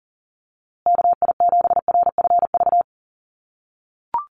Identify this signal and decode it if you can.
Morse code, heard as a keyed tone, about 40 wpm, 700 Hz. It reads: KI7RFV
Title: USA Call Signs Repeated 2x 40wpm